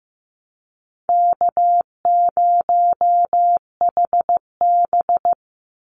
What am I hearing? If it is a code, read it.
Morse code, heard as a keyed tone, about 15 wpm, 700 Hz. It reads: K0HB
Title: Easy Call Signs - 15wpm